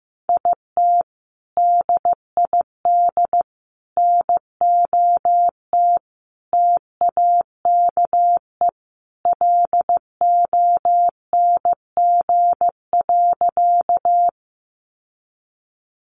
Morse code, heard as a keyed tone, about 15 wpm, 700 Hz. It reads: IT DID NOT TAKE LONG.
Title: Sentences from Top 600 Words 15wpm